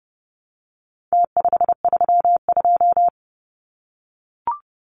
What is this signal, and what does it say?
Morse code, heard as a keyed tone, about 30 wpm, 700 Hz. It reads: T532
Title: Numbers - Contest Style - 4-Digit 30wpm